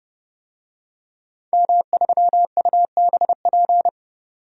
Morse code, heard as a keyed tone, about 30 wpm, 700 Hz. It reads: M3UBP
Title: DXCC-100 Call Signs - 5 Characters - 30wpm